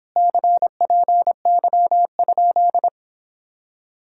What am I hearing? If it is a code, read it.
Morse code, heard as a keyed tone, about 26 wpm, 700 Hz. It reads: CPY?